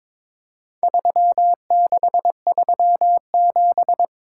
Morse code, heard as a keyed tone, about 22 wpm, 700 Hz. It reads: 3637
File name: practice / 22wpm / Four Letter-Number Combinations - 22wpm